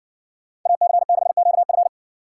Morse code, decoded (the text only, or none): AC6QF